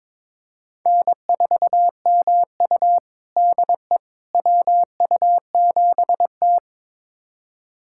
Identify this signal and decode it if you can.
Morse code, heard as a keyed tone, about 21 wpm, 700 Hz. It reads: N4MU DE WU7T
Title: USA Callsign DE Callsign - 4 Characters - 22wpm